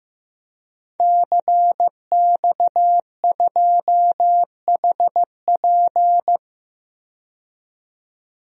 Morse code, heard as a keyed tone, about 15 wpm, 700 Hz. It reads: CX2HP
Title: DXCC-100 Call Signs - 5 Characters - 15wpm